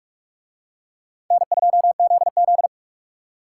Morse code, heard as a keyed tone, about 37 wpm, 700 Hz. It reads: N2ZZ
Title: Speed Racing - USA 1x2 Call Signs 30wpm